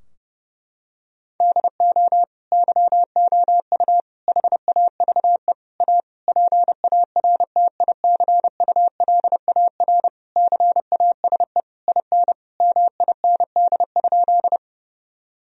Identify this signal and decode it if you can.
Morse code, heard as a keyed tone, about 32 wpm, 700 Hz. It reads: DO YOU HAVE A PARTICULAR CASE IN MIND?
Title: Sentences from Top 1000 Words - Review Part 4 - 30wpm